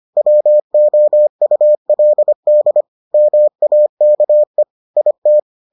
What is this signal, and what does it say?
Morse code, heard as a keyed tone, about 25 wpm, 600 Hz. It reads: WOULD MAKE IT